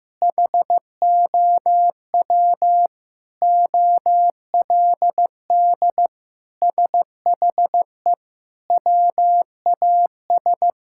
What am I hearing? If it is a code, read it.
Morse code, heard as a keyed tone, about 15 wpm, 700 Hz. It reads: HOW OLD SHE WAS
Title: Sentences from Top 1600 Words 15wpm